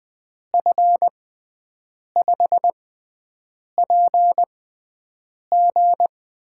Morse code, heard as a keyed tone, about 20 wpm, spaced as for 7 wpm, 700 Hz. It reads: F5PG
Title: USA Call Signs - 6 Characters - 6x Character Spacing - 20wpm